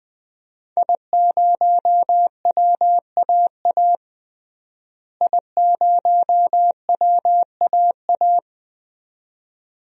Morse code, heard as a keyed tone, about 20 wpm, 700 Hz. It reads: I0WAA I0WAA
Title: DXCC-100 Call Signs - 5 Characters - Repeated 2x - 20wpm